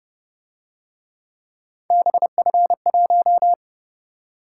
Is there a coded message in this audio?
Yes: Morse code from 1.9 to 3.5 s, heard as a keyed tone, about 30 wpm, 700 Hz. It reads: BF1